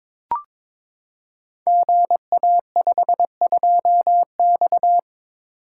Morse code, heard as a keyed tone, about 22 wpm, 700 Hz. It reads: GA52X
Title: Code Groups - 5 Letter:Numbers - 22wpm